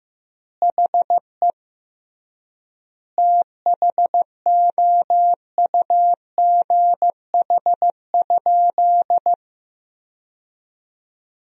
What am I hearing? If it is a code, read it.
Morse code, heard as a keyed tone, about 15 wpm, 700 Hz. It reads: HE THOUGH?
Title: 3x Word-Spacing - Sentences from Top 400 Words - 15wpm